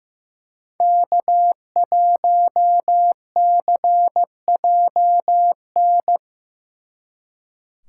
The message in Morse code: K1CJN

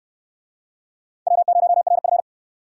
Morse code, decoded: W0RF

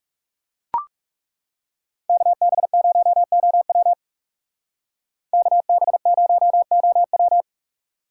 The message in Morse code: KB0OW KB0OW

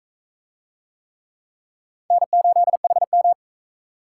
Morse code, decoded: N8HM

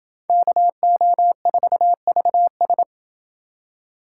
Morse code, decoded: KO4VS

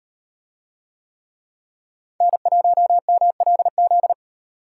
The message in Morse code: N1MLZ

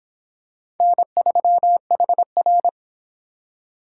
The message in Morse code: N3HR

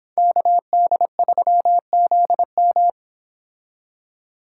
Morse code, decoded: KD3ZM

